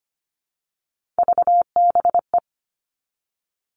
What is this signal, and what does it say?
Morse code, heard as a keyed tone, about 25 wpm, 700 Hz. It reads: VBE
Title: Three Challenging Letter-Number Combinations 25wpm